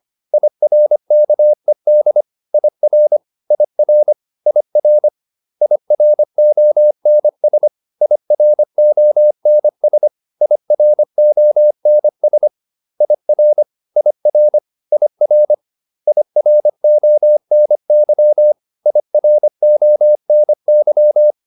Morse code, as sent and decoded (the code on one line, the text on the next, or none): .. .-. -.- . -.. / .. .-. / .. .-. / .. .-. / .. .-. --- -. ... / .. .-. --- -. ... / .. .-. --- -. ... / .. .-. / .. .-. / .. .-. / .. .-. --- -. -.-- / .. .-. --- -. -.--
IRKED IR IR IR IRONS IRONS IRONS IR IR IR IRONY IRONY